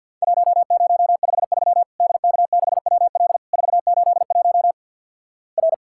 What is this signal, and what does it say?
Morse code, heard as a keyed tone, about 53 wpm, 700 Hz. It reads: 1053 DK6WL 481 R
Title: Worked All Europe DX Contest - QTC Exchange with Repeats - 50wpm